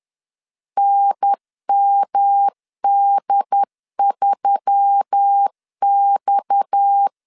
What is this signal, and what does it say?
Morse code, heard as a keyed tone, about 10 wpm, 800 Hz. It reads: NMD3X